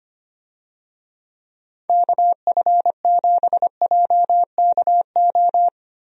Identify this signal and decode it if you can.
Morse code, heard as a keyed tone, about 25 wpm, 700 Hz. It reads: KF7JKO
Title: USA Call Signs - 6 Characters - Repeated 2x - 25wpm